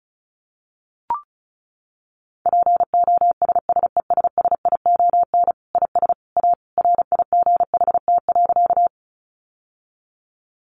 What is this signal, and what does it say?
Morse code, heard as a keyed tone, about 34 wpm, 700 Hz. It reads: POSSESSION IS A RIGHT.
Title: Sentences from Top 1400 Words 35wpm